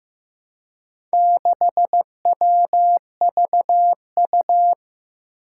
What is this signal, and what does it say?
Morse code, heard as a keyed tone, about 15 wpm, 700 Hz. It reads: 6WVU
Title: Four Challenging Letter-Number Combinations 15wpm